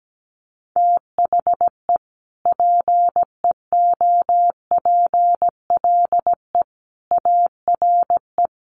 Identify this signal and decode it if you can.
Morse code, heard as a keyed tone, about 17 wpm, 700 Hz. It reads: THE PEOPLE ARE
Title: Sentences from Top 1000 Words 17wpm